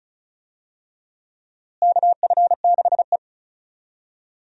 Morse code, heard as a keyed tone, about 34 wpm, 700 Hz. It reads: KF6E
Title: USA Call Signs - 4 Characters - Repeated 2x - 35wpm